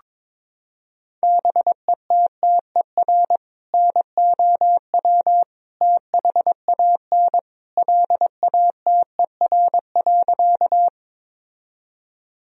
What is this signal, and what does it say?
Morse code, heard as a keyed tone, about 22 wpm, 700 Hz. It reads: BETTER NOW THAN LATER.
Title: Sentences from Top 1000 Words - Review Part 4 - 22wpm